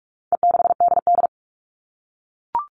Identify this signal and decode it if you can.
Morse code, heard as a keyed tone, about 44 wpm, 700 Hz. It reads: E6DD
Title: All Cut Numbers - 4-Digit 45wpm